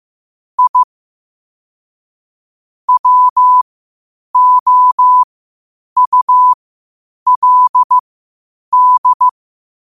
Morse code, heard as a keyed tone, about 15 wpm, spaced as for 8 wpm, 1000 Hz. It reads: I WOULD
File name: qotdfiles / 8wpm